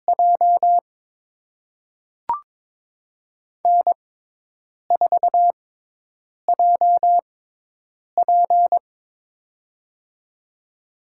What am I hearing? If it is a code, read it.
Morse code, heard as a keyed tone, about 22 wpm, spaced as for 8 wpm, 700 Hz. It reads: J N4JP
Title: USA Call Signs - 6x Character Spacing - 22wpm